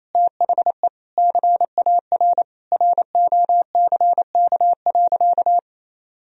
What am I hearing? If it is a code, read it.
Morse code, heard as a keyed tone, about 28 wpm, 700 Hz. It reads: THE CAR ROCK.
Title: Sentences from Top 1000 Words - Review Part 1 - 28wpm